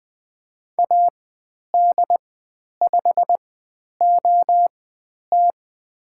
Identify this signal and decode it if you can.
Morse code, heard as a keyed tone, about 20 wpm, spaced as for 10 wpm, 700 Hz. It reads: AD5OT
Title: 102 Lesson 17g - Letter u - Call Signs 20wpm